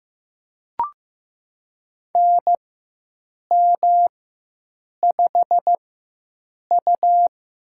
Morse code, heard as a keyed tone, about 16 wpm, spaced as for 7 wpm, 700 Hz. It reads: NM5U
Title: USA Call Signs - 4 Characters - 4x Character Spacing - 15wpm